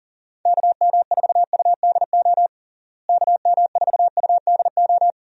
Morse code, heard as a keyed tone, about 39 wpm, 700 Hz. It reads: KM4UDO KM4UDO
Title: USA Call Signs - 6 Characters - Repeated 2x - 40wpm